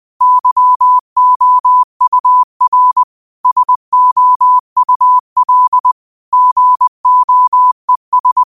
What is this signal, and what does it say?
Morse code, heard as a keyed tone, about 20 wpm, 1000 Hz. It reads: YOUR SOUL GOES
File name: qotdfiles / 20wpm